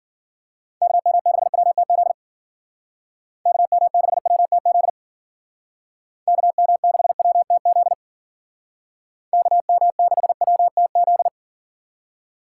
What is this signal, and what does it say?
Morse code, heard as a keyed tone, about 48 wpm, 700 Hz. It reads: KM6WTZ KM6WTZ KM6WTZ KM6WTZ
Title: Speed Racing - USA Call Signs 40wpm